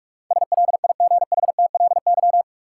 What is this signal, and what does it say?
Morse code, heard as a keyed tone, about 47 wpm, 700 Hz. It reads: SLIGHTLY
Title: Sentences from Top 1300 Words - New Word Drill Alt - 45wpm